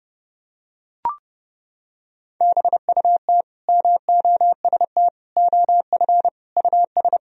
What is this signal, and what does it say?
Morse code, heard as a keyed tone, about 30 wpm, 700 Hz. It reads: BUT MOST OF US
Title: Sentences from Top 1000 Words - Review Part 1 - 30wpm